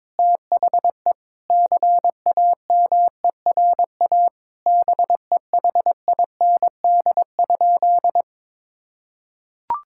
Morse code, heard as a keyed tone, about 22 wpm, 700 Hz. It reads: THE CAMERA BEHIND?
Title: Sentences from Top 1000 Words - Review Part 5 - 22wpm